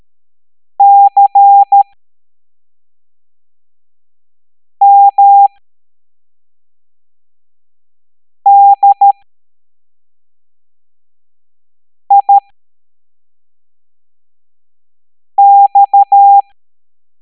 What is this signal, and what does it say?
Morse code, heard as a keyed tone, about 13 wpm, spaced as for 3 wpm, 800 Hz. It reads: CMDIX